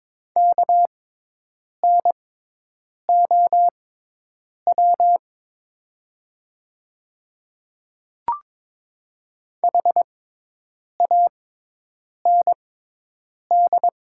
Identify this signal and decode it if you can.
Morse code, heard as a keyed tone, about 22 wpm, spaced as for 8 wpm, 700 Hz. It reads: KNOW HAND
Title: 6x Character Spacing - Top 300 Words - 22wpm